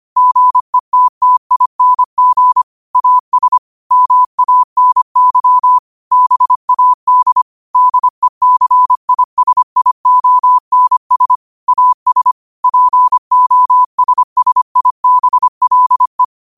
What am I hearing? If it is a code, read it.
Morse code, heard as a keyed tone, about 25 wpm, 1000 Hz. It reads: GETTING AS MANY BAD DECISIONS AS POSSIBLE
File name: qotdfiles / 25wpm